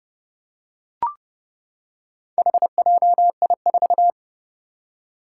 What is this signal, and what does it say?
Morse code, heard as a keyed tone, about 30 wpm, 700 Hz. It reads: HJI4
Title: Four Challenging Letter-Number Combinations 30wpm